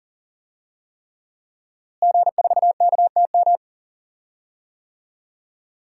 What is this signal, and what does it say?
Morse code, heard as a keyed tone, about 38 wpm, 700 Hz. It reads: G4KTM
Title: DXCC-100 Call Signs - 5 Characters - Repeated 3x - 40wpm